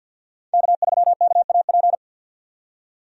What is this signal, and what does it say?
Morse code, heard as a keyed tone, about 49 wpm, 700 Hz. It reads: K3KAP
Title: USA Call Signs - 5 Characters - 50wpm